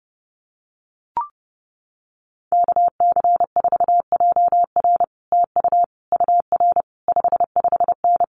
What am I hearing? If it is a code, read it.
Morse code, heard as a keyed tone, about 30 wpm, 700 Hz. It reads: KC4JR TU UR 55N
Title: Common QSO Phrases Encore 30wpm